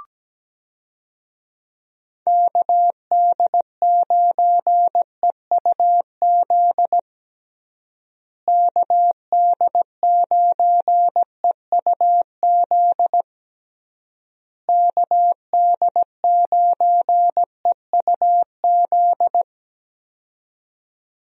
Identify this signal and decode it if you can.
Morse code, heard as a keyed tone, about 17 wpm, 700 Hz. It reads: KD9EUZ KD9EUZ KD9EUZ
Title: USA Call Signs - 6 Characters - Repeated 3x - 17wpm